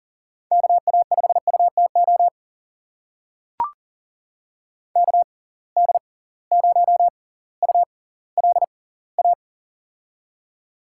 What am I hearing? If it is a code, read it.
Morse code, heard as a keyed tone, about 40 wpm, spaced as for 20 wpm, 700 Hz. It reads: KA5UTO KD0ULA
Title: USA Call Signs - 6 Characters - 6x Character Spacing - 40wpm